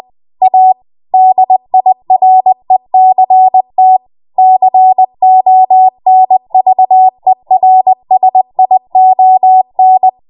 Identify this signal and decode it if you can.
Morse code, heard as a keyed tone, about 20 wpm, 750 Hz. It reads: A DIRECT CONVERSION
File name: Morse / 20WPM